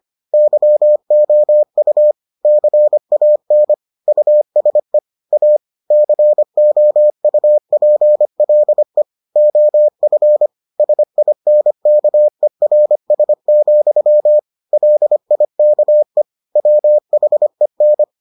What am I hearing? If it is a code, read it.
Morse code, heard as a keyed tone, about 25 wpm, 600 Hz. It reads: YOU CAN USE A COUPLE OF SINKERS, LIKE WHEN